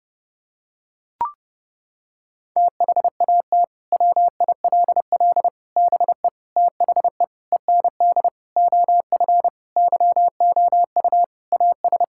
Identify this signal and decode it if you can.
Morse code, heard as a keyed tone, about 30 wpm, 700 Hz. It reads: THAT WILL BE THE END OF YOU AS
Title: Sentences from Top 1000 Words - Review Part 6 - 30wpm